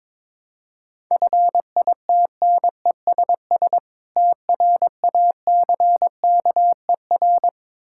Morse code, heard as a keyed tone, about 21 wpm, 690 Hz. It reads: FITNESS TRACKER
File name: Difficult Words with Hints - 22wpm